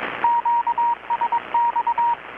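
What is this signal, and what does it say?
Morse code, heard as a keyed tone, about 22 wpm, 940 Hz, with radio noise. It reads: QSX
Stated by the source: Morse from KPH's Night of Nights XX maritime radiotelegraphy event, received in Chiapas, Mexico about 3,800 km away on a Kenwood R-600 with a grounded 10 m long-wire antenna.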